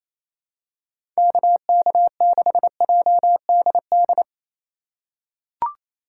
Morse code, heard as a keyed tone, about 28 wpm, 700 Hz. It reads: KK6JDD
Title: USA Call Signs - 3x Character Spacing - 28wpm